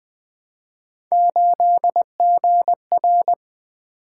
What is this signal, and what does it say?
Morse code, heard as a keyed tone, about 20 wpm, 700 Hz. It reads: 8GR
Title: Three Letter-Number Combinations - 20wpm